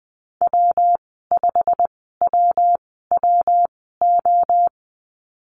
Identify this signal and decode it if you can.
Morse code, heard as a keyed tone, about 20 wpm, spaced as for 14 wpm, 700 Hz. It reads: W5WWO
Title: 161 Lesson 26g - Letter g - Call Signs 20wpm